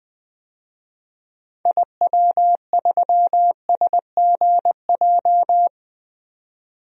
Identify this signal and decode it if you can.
Morse code, heard as a keyed tone, about 20 wpm, 700 Hz. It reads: IW3SGJ
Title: DXCC-100 Call Signs - 6 Characters - 20wpm